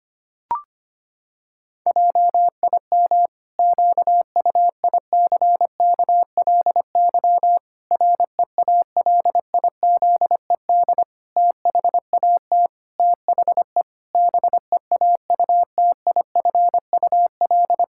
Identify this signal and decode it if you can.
Morse code, heard as a keyed tone, about 25 wpm, 700 Hz. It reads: JIM QUICKLY REALIZED THAT THE BEAUTIFUL
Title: Pangram Challenge v2 - 25wpm